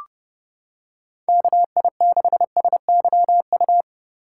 Morse code, heard as a keyed tone, about 31 wpm, 700 Hz. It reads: KI6SYU